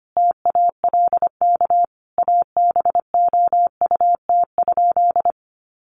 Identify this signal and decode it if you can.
Morse code, heard as a keyed tone, about 25 wpm, 700 Hz. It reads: TALK ABOUT?
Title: Sentences from Top 1100 Words 25wpm